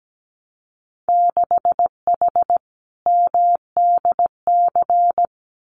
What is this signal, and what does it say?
Morse code, heard as a keyed tone, about 17 wpm, 700 Hz. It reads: 6H MDC
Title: Winter Field Day Exchanges Encore 17wpm